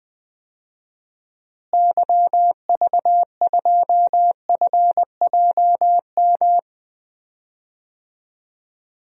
Morse code, heard as a keyed tone, about 20 wpm, 700 Hz. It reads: YV2FJM
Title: DXCC-100 Call Signs - 6 Characters - Repeated 2x - 20wpm